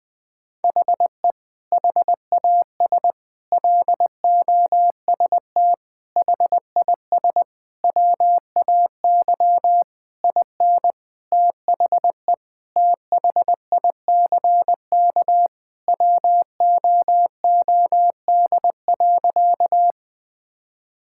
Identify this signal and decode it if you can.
Morse code, heard as a keyed tone, about 20 wpm, 700 Hz. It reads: HE HAS LOST HIS WAY IN THE THICK WOOD.
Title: Sentences from Top 1600 Words 20wpm